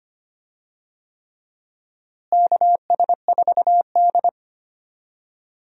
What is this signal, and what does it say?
Morse code, heard as a keyed tone, about 25 wpm, 700 Hz. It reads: KS4D
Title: USA Call Signs - 4 Characters - Repeated 3x - 25wpm